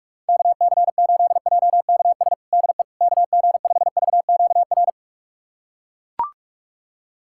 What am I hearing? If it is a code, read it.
Morse code, heard as a keyed tone, about 44 wpm, 700 Hz. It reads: KC8JKS DE KG5VQR
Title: USA Callsign DE Callsign - 6 Characters - 45wpm